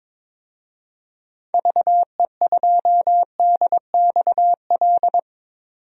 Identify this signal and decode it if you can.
Morse code, heard as a keyed tone, about 22 wpm, 700 Hz. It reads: VE2DXL
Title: DXCC-100 Call Signs - 6 Characters - 22wpm